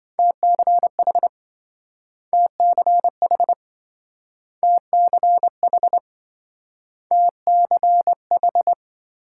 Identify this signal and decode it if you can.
Morse code, heard as a keyed tone, about 27 wpm, 700 Hz. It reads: TCH TCH TCH TCH